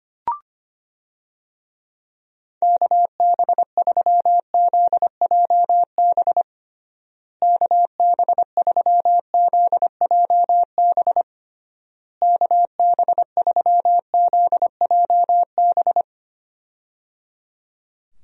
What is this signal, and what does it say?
Morse code, heard as a keyed tone, about 25 wpm, 700 Hz. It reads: KB3ZJB KB3ZJB KB3ZJB
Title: USA Call Signs - 6 Characters - Repeated 3x - 25wpm